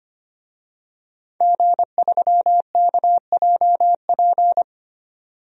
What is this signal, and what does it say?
Morse code, heard as a keyed tone, about 25 wpm, 700 Hz. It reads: G3KJP
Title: DXCC-100 Call Signs - 5 Characters - 25wpm